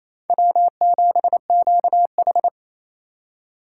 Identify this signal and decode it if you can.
Morse code, heard as a keyed tone, about 28 wpm, 700 Hz. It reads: W7QH
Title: USA Call Signs - 3x Character Spacing - 28wpm